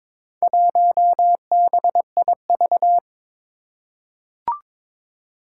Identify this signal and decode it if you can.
Morse code, heard as a keyed tone, about 22 wpm, 700 Hz. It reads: 1BIV
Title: Four Challenging Letter-Number Combinations 22wpm